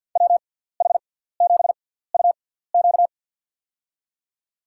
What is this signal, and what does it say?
Morse code, heard as a keyed tone, about 47 wpm, spaced as for 17 wpm, 700 Hz. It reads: WH7UQ
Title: USA Call Signs - 5 Characters - 6x Character Spacing - 50wpm